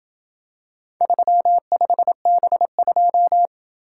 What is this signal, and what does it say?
Morse code, heard as a keyed tone, about 27 wpm, 700 Hz. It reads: 35B2